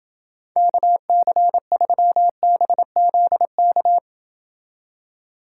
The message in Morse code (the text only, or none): KC3BZK